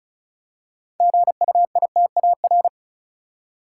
Morse code, heard as a keyed tone, about 36 wpm, 700 Hz. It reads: GUITAR
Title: Six Letter Words - 35wpm